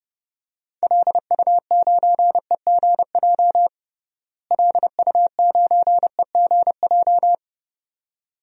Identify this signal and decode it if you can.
Morse code, heard as a keyed tone, about 32 wpm, 700 Hz. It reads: LU9EGJ LU9EGJ
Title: DXCC-100 Call Signs - 6 Characters - Repeated 2x - 30wpm